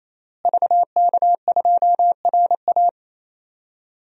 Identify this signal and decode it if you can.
Morse code, heard as a keyed tone, about 28 wpm, 700 Hz. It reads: VK2RA
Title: DXCC-100 Call Signs - 5 Characters - Repeated 3x - 28wpm